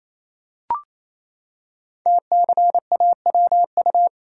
Morse code, heard as a keyed tone, about 28 wpm, 700 Hz. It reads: TCAWU
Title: Code Groups - 5 Letter:Numbers - 28wpm